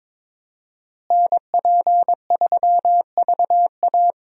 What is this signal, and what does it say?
Morse code, heard as a keyed tone, about 22 wpm, 700 Hz. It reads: NP3VA
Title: USA Call Signs - 5 Characters - Repeated 2x - 22wpm